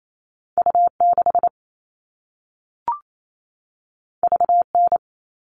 Morse code, heard as a keyed tone, about 28 wpm, 700 Hz. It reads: U6 VN
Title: All Cut Numbers - 2-Digit 28wpm